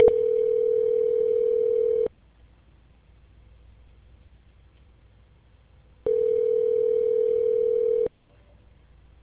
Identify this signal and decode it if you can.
Ringback tone (North America), 440 + 480 Hz.